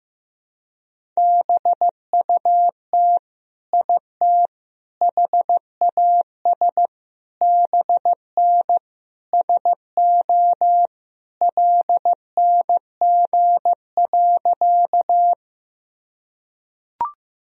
Morse code, heard as a keyed tone, about 15 wpm, 700 Hz. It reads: BUT IT HAS BN SO LNG.
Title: Sentences from Top 100 Words and Abbreviations 15wpm